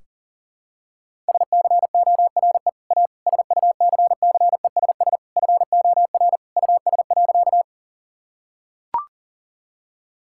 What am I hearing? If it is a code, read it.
Morse code, heard as a keyed tone, about 40 wpm, 700 Hz. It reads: SCORE A SUCCESS FOR US.
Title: Sentences from Top 1600 Words 40wpm